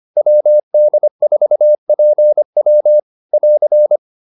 Morse code, heard as a keyed tone, about 25 wpm, 600 Hz. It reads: WD4PW <AR>